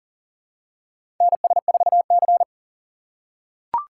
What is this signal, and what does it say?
Morse code, heard as a keyed tone, about 40 wpm, 700 Hz. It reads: NS4C
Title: USA Call Signs - 4 Characters - 4x Character Spacing - 40wpm